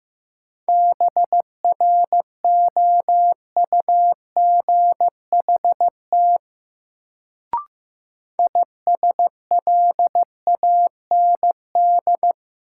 Morse code, heard as a keyed tone, about 16 wpm, 700 Hz. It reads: BROUGHT ISLAND
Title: Sentences from Top 500 Words - New Word Drill - 15wpm